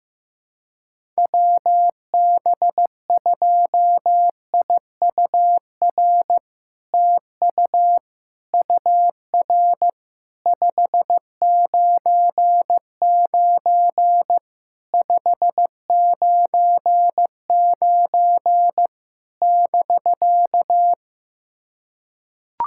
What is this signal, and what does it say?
Morse code, heard as a keyed tone, about 16 wpm, 700 Hz. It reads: WB2IUR TU UR 599 599 <BK>
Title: SOTA QSO Practice 15wpm